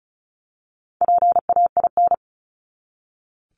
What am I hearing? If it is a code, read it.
Morse code, heard as a keyed tone, about 34 wpm, 700 Hz. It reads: PAIN
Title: Top 1501-1750 Words 35wpm